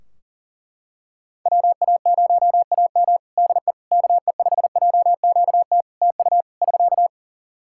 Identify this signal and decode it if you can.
Morse code, heard as a keyed tone, about 40 wpm, 700 Hz. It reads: WA0AM DE KE5JQT TU <SK>
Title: Practice QSO - Exchange 4 - Ending - 40wpm